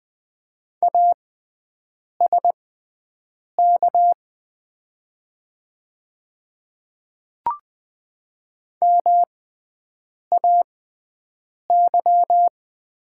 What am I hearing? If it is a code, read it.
Morse code, heard as a keyed tone, about 20 wpm, spaced as for 7 wpm, 700 Hz. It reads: ASK MAY
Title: 6x Character Spacing - Top 300 Words - 20wpm